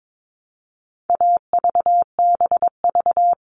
Morse code, heard as a keyed tone, about 22 wpm, 700 Hz. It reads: AVBV
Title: All Cut Numbers - 4-Digit 22wpm